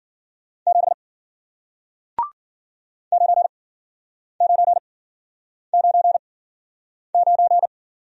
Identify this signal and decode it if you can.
Morse code, heard as a keyed tone, about 55 wpm, spaced as for 9 wpm, 700 Hz. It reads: 7 9999